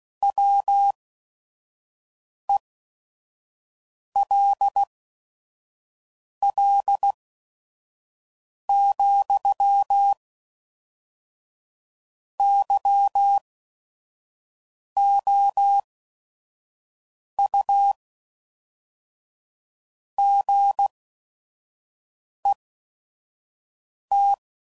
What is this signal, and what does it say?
Morse code, heard as a keyed tone, about 16 wpm, spaced as for 5 wpm, 800 Hz. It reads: WELL, YOU GET
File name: CW Code Groups @5WPM / Technical Subjects / AND Gate